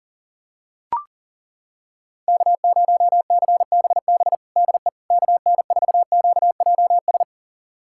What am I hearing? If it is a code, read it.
Morse code, heard as a keyed tone, about 40 wpm, 700 Hz. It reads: K0CBB DE KN4QJS
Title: USA Callsign DE Callsign 40wpm